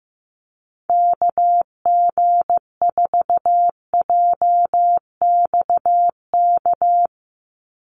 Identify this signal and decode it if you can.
Morse code, heard as a keyed tone, about 15 wpm, 700 Hz. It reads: KG4JXK
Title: USA Call Signs 15wpm